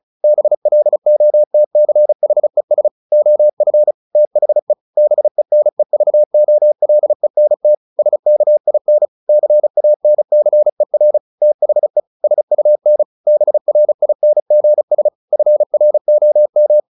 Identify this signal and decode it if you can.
Morse code, heard as a keyed tone, about 34 wpm, 600 Hz. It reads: BLOTCHES OF THE BENEVOLENT SKIN CANCER THE SUN BRINGS FROM
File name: the old man and the sea / the old man and the sea 1 35wpm